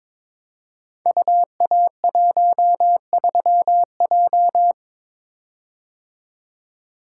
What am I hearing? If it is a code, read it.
Morse code, heard as a keyed tone, about 23 wpm, 700 Hz. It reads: UA13J
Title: Code Groups - 5 Letter:Numbers - 22wpm